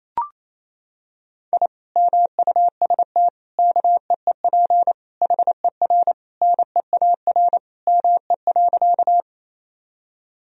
Morse code, heard as a keyed tone, about 28 wpm, 700 Hz. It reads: I MUST KEEP HER NEAR ME.
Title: Sentences from Top 1000 Words - Review Part 1 - 28wpm